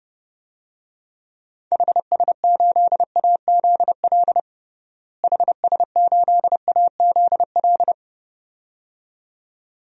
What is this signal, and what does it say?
Morse code, heard as a keyed tone, about 30 wpm, 700 Hz. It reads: HS8AZL HS8AZL
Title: DXCC-100 Call Signs - 6 Characters - Repeated 2x - 30wpm